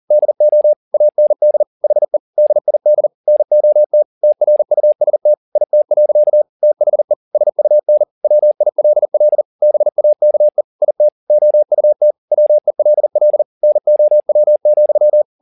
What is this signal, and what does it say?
Morse code, heard as a keyed tone, about 40 wpm, 600 Hz. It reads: DO AND HE DID NOT TRUST IT. THE SUN WILL BAKE IT OUT WELL NOW,
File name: the old man and the sea 2 40wpm